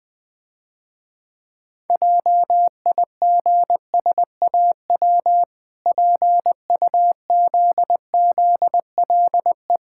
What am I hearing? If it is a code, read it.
Morse code, heard as a keyed tone, about 21 wpm, 700 Hz. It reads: JIGSAW PUZZLE
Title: Easy Words - Speed-Racing with Hints - 20wpm